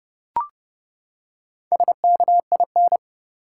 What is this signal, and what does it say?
Morse code, heard as a keyed tone, about 30 wpm, 700 Hz. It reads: SKIN
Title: Four Letter Words - 30wpm